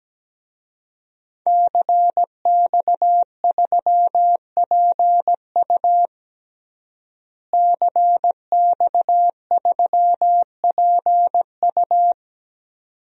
Morse code, heard as a keyed tone, about 17 wpm, 700 Hz. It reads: CX3PU CX3PU
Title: DXCC-100 Call Signs - 5 Characters - Repeated 2x - 17wpm